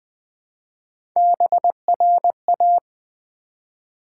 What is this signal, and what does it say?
Morse code, heard as a keyed tone, about 20 wpm, 700 Hz. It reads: BRA
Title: Build A Word - 1 Letter At A Time - 5-Letter Words - 20wpm